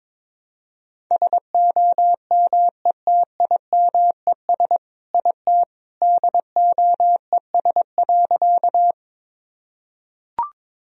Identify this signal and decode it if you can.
Morse code, heard as a keyed tone, about 22 wpm, 700 Hz. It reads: SOMETIMES IT DOES.
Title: Sentences from Top 1000 Words - Review Part 3 - 22wpm